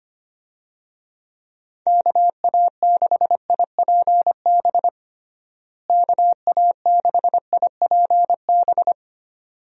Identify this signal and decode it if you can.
Morse code, heard as a keyed tone, about 25 wpm, 700 Hz. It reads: KA6IPB KA6IPB
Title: USA Call Signs - 6 Characters - Repeated 2x - 25wpm